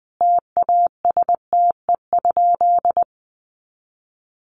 Morse code, heard as a keyed tone, about 19 wpm, 700 Hz. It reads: TASTE?
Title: Sentences from Top 1100 Words 20wpm